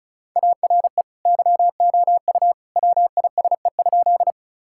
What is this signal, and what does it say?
Morse code, heard as a keyed tone, about 36 wpm, 700 Hz. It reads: ARE YOU WISE?
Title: Sentences from Top 1600 Words 35wpm